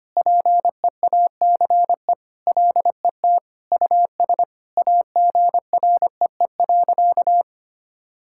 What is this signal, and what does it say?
Morse code, heard as a keyed tone, about 25 wpm, 700 Hz. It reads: PEACE LET US AGREE.
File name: practice / 25wpm / Sentences with 4-Word Phrases - 25wpm